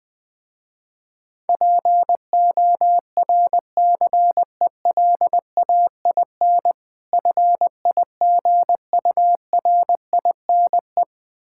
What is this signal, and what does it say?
Morse code, heard as a keyed tone, about 20 wpm, 700 Hz. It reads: PORCELAIN FIGURINE
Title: Difficult Words with Hints - 20wpm